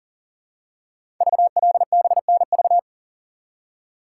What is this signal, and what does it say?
Morse code, heard as a keyed tone, about 41 wpm, 690 Hz. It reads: VLBNV
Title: Code Groups - 5 Letter:Numbers - 40wpm